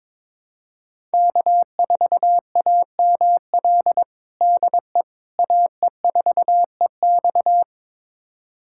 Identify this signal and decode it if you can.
Morse code, heard as a keyed tone, about 22 wpm, 700 Hz. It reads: K4AML DE AE4EX
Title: USA Callsign DE Callsign - 5 Characters - 22wpm